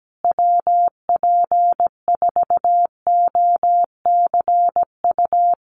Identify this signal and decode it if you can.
Morse code, heard as a keyed tone, about 17 wpm, 700 Hz. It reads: WP4OCU
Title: USA Call Signs Repeated 3x 17wpm